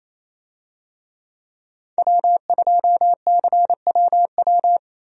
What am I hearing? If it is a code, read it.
Morse code, heard as a keyed tone, about 28 wpm, 700 Hz. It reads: W2CWW
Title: USA Call Signs - 5 Characters - 28wpm